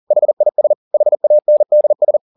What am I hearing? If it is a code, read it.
Morse code, heard as a keyed tone, about 40 wpm, 600 Hz. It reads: HIS HANDS
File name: the old man and the sea 2 40wpm